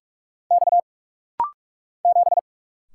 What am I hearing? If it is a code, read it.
Morse code, heard as a keyed tone, about 46 wpm, 700 Hz. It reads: X 7
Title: Single Letter-Number Alt - Rapid-Fire 45wpm